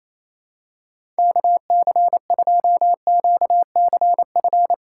Morse code, heard as a keyed tone, about 28 wpm, 700 Hz. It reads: KC2QCF
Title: USA Call Signs - 6x Character Spacing - 28wpm